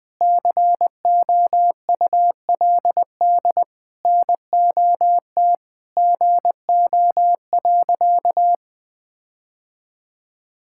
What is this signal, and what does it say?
Morse code, heard as a keyed tone, about 21 wpm, 700 Hz. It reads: COULD NOT GO.
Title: Sentences with 5-Word Phrases - Spoken Beginning - 20wpm